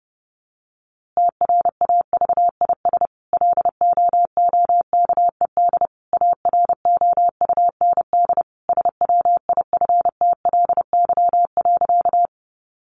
Morse code, heard as a keyed tone, about 30 wpm, 700 Hz. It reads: TRAVIS LOOKED AROUND SWIFTLY.
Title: A Sound Of Thunder 30wpm